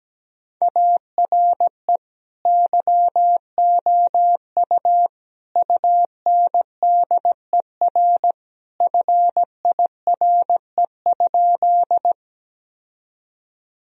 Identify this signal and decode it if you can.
Morse code, heard as a keyed tone, about 17 wpm, 700 Hz. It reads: ARE YOU UNDER FIRE?
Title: Sentences from Top 1000 Words - Review Part 1 - 17wpm